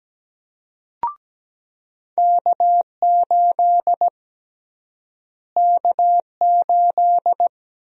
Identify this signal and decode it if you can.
Morse code, heard as a keyed tone, about 17 wpm, 700 Hz. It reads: K8 K8
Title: USA Call Sign Prefixes - Repeated 2x - 17wpm